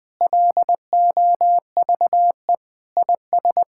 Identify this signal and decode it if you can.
Morse code, heard as a keyed tone, about 21 wpm, 700 Hz. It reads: LOVE IS
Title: Sentences with 3-Word Phrases - 20wpm